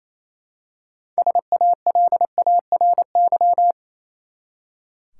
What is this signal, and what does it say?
Morse code, heard as a keyed tone, about 28 wpm, 700 Hz. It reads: SALARY
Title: Six Letter Words - 28wpm